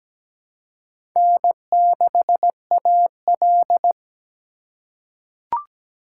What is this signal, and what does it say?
Morse code, heard as a keyed tone, about 17 wpm, 700 Hz. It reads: N6AL
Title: USA Call Signs - 2x Character Spacing - 17wpm